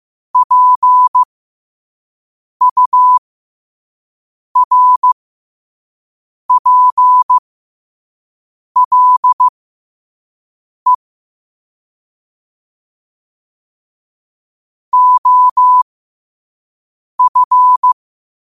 Morse code, heard as a keyed tone, about 15 wpm, spaced as for 5 wpm, 1000 Hz. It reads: PURPLE OF
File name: qotdfiles / 5wpm